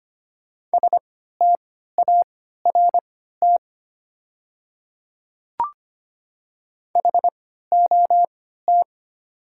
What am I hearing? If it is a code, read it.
Morse code, heard as a keyed tone, about 25 wpm, spaced as for 14 wpm, 690 Hz. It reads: START HOT
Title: 3x Character Spacing - Top 500 Words - 25wpm